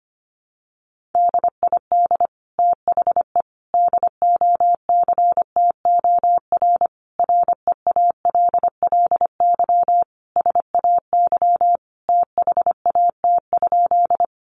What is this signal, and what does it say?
Morse code, heard as a keyed tone, about 25 wpm, 700 Hz. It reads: DID THE DOCTOR REALLY SAY THAT?
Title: Sentences from Top 1500 Words 25wpm